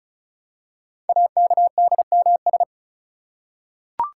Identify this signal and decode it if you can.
Morse code, heard as a keyed tone, about 36 wpm, 700 Hz. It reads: AKDMS
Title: Code Groups - 5 Letters - 35wpm